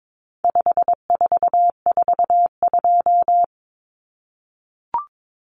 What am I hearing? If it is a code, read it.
Morse code, heard as a keyed tone, about 22 wpm, 700 Hz. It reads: 5442
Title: Numbers - 4-Digit 22wpm